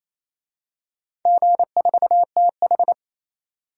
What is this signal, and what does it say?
Morse code, heard as a keyed tone, about 28 wpm, 700 Hz. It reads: G4TH
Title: DXCC-100 Call Signs - 4 Characters - 28wpm